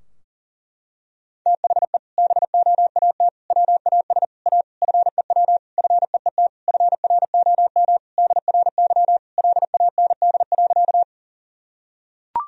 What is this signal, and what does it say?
Morse code, heard as a keyed tone, about 40 wpm, 700 Hz. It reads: THE BOAT WAS A FEW FEET FROM DRY LAND.
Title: Sentences from Top 1000 Words - Review Part 3 - 40wpm